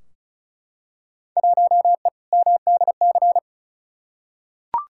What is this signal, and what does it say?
Morse code, heard as a keyed tone, about 35 wpm, 690 Hz. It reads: 1E MDC
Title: Field Day Exchanges Encore - 35wpm